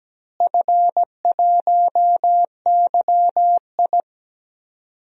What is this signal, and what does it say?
Morse code, heard as a keyed tone, about 17 wpm, 700 Hz. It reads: F1YI
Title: DXCC-100 Call Signs - 4 Characters - Repeated 3x - 17wpm